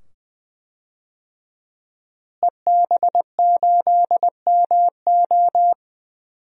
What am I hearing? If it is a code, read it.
Morse code, heard as a keyed tone, about 20 wpm, 700 Hz. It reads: EB8MO
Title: DXCC-100 Call Signs - 5 Characters - Repeated 2x - 20wpm